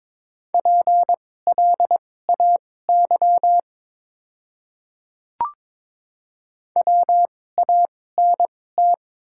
Morse code, heard as a keyed tone, about 22 wpm, spaced as for 16 wpm, 700 Hz. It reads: PLAY WANT
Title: 2x Character Spacing - Top 300 Words - 22wpm